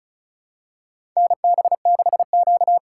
Morse code, heard as a keyed tone, about 34 wpm, 700 Hz. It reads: NB6Q
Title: USA Call Signs - 4 Characters - 4x Character Spacing - 35wpm